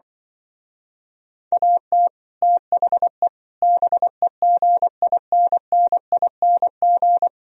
Morse code, heard as a keyed tone, about 25 wpm, 700 Hz. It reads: AT THE BEGINNING
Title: Speed-Racing - Sets of 4 Words 20wpm